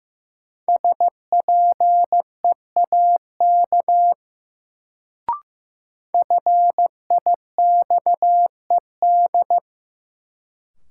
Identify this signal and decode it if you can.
Morse code, heard as a keyed tone, about 15 wpm, 700 Hz. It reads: SPEAK FIXED
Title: Sentences from Top 1100 Words - New Word Drill Alt - 15wpm